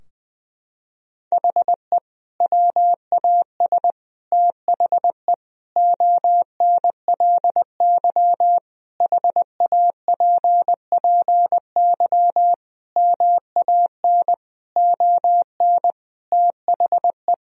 Morse code, heard as a keyed tone, about 20 wpm, 700 Hz. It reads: HE WAS THE ONLY HAPPY MAN ON THE
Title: Sentences from Top 1000 Words - Review Part 3 - 20wpm